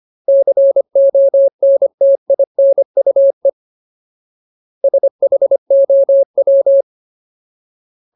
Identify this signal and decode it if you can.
Morse code, heard as a keyed tone, about 25 wpm, 550 Hz. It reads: CONTINUE SHOW
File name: Ultimate Word List / 25WPM Ultimate